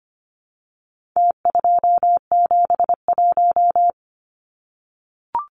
Morse code, heard as a keyed tone, about 25 wpm, 700 Hz. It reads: T271
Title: Numbers - Contest Style - 4-Digit 25wpm